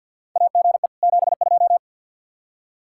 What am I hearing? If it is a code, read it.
Morse code, heard as a keyed tone, about 50 wpm, 700 Hz. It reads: AGE 72
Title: K1USN SST QSO Practice 50wpm